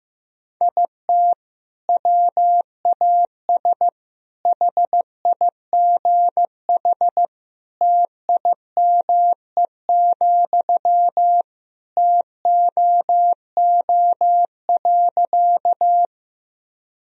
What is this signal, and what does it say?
Morse code, heard as a keyed tone, about 15 wpm, 700 Hz. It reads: IT WAS HIGH TIME, TOO.
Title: Sentences from Top 1000 Words - Review Part 1 - 15wpm